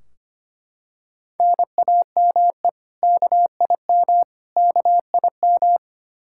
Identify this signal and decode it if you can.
Morse code, heard as a keyed tone, about 25 wpm, 700 Hz. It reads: NAME KIM KIM
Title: K1USN SST QSO Practice 25wpm